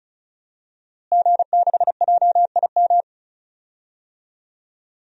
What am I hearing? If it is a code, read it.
Morse code, heard as a keyed tone, about 36 wpm, 700 Hz. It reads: G6JIM
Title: DXCC-100 Call Signs - 5 Characters - Repeated 2x - 35wpm